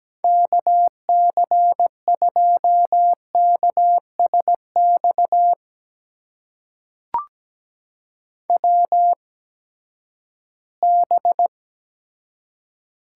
Morse code, heard as a keyed tone, about 17 wpm, spaced as for 12 wpm, 700 Hz. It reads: KC2KSX WB
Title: USA Call Signs - 6 Characters - 8x Character Spacing - 17wpm